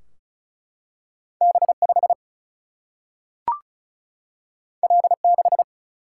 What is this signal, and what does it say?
Morse code, heard as a keyed tone, about 36 wpm, 700 Hz. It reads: B 5 L 6
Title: Single-Letter Single-Number 35wpm